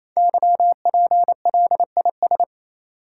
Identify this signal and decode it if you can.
Morse code, heard as a keyed tone, about 28 wpm, 700 Hz. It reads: YPLIS
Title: Code Groups - 5 Letters - 28wpm